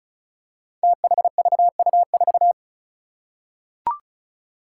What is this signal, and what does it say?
Morse code, heard as a keyed tone, about 36 wpm, 700 Hz. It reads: THVU4
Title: Five Challenging Letter-Number Combinations 35wpm